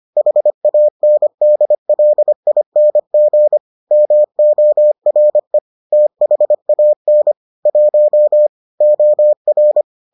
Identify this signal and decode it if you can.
Morse code, heard as a keyed tone, about 25 wpm, 600 Hz. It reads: HANDLING MORE THAN 1 OR